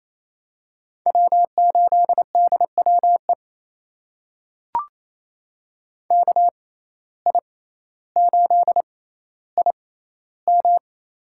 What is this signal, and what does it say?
Morse code, heard as a keyed tone, about 27 wpm, spaced as for 14 wpm, 700 Hz. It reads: W8DWE KI8IM
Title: USA Call Signs - 5 Characters - 6x Character Spacing - 28wpm